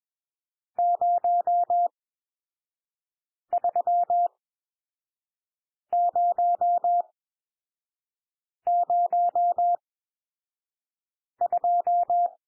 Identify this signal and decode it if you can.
Morse code, heard as a keyed tone, about 21 wpm, spaced as for 5 wpm, 700 Hz. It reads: 03002
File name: cwmp3 / N1 SOUND NUMBERS 21 wpm